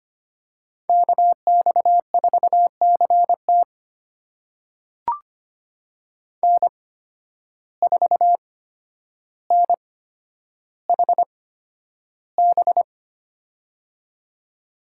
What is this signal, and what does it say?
Morse code, heard as a keyed tone, about 25 wpm, spaced as for 11 wpm, 700 Hz. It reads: KX4CT N4NHB
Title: USA Call Signs - 5 Characters - 8x Character Spacing - 25wpm